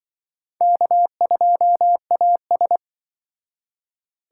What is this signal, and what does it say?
Morse code, heard as a keyed tone, about 24 wpm, 700 Hz. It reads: K2AS
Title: Speed Racing - USA 1x2 Call Signs 20wpm